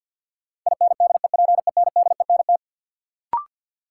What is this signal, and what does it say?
Morse code, heard as a keyed tone, about 50 wpm, 700 Hz. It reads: INDEPENDENT
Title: Sentences from Top 900 Words - New Word Drill - 50wpm